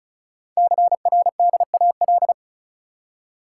Morse code, heard as a keyed tone, about 36 wpm, 700 Hz. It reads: CRDAL
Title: Code Groups - 5 Letters - 35wpm